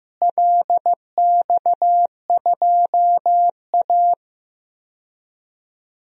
Morse code, heard as a keyed tone, about 15 wpm, 700 Hz. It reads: LX2A
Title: DXCC-100 Call Signs - 4 Characters - Repeated 2x - 15wpm